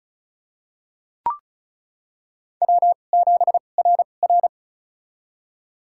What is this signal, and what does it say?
Morse code, heard as a keyed tone, about 35 wpm, spaced as for 25 wpm, 690 Hz. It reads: W7RR
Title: USA Call Signs - 4 Characters - 2x Character Spacing - 35wpm